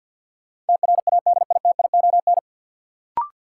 Morse code, heard as a keyed tone, about 53 wpm, 700 Hz. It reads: TRADITION
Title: Sentences from Top 1500 Words - New Word Drill - 50wpm